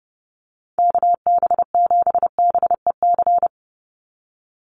Morse code, heard as a keyed tone, about 30 wpm, 700 Hz. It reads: KB7BEC
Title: USA Call Signs 30wpm